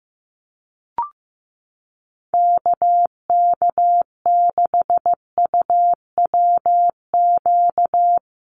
Morse code, heard as a keyed tone, about 15 wpm, 700 Hz. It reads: KK6UWQ
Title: USA Call Signs 15wpm